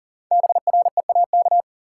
Morse code, heard as a keyed tone, about 40 wpm, 700 Hz. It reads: BREAK
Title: Sets of 4 Words - New Word Drill - 40wpm